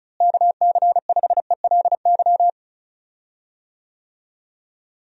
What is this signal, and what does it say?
Morse code, heard as a keyed tone, about 36 wpm, 700 Hz. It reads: KC5ELY
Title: USA Call Signs - 6 Characters - Repeated 2x - 35wpm